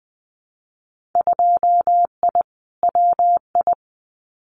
Morse code, heard as a keyed tone, about 20 wpm, 700 Hz. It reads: 2I WI
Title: Winter Field Day Exchanges 20wpm